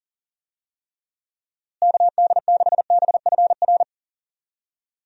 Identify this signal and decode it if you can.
Morse code, heard as a keyed tone, about 40 wpm, 700 Hz. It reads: KD6BFR
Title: USA Call Signs - 6 Characters - Repeated 2x - 40wpm